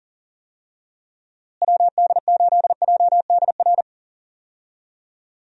Morse code, heard as a keyed tone, about 40 wpm, 700 Hz. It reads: WD8JDR
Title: USA Call Signs - 6 Characters - Repeated 3x - 40wpm